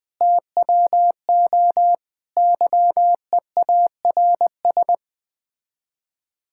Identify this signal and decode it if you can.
Morse code, heard as a keyed tone, about 20 wpm, 700 Hz. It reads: TWO YEARS
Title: Build A Phrase - 5-Word Phrases - 20wpm